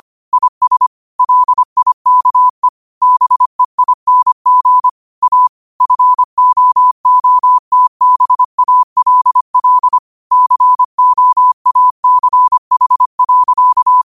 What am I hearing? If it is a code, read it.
Morse code, heard as a keyed tone, about 25 wpm, 1000 Hz. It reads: IS LIKE BEING A FOOTBALL COACH.